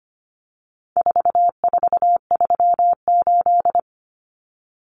Morse code, heard as a keyed tone, about 25 wpm, 700 Hz. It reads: 4438
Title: Numbers - 4-Digit 25wpm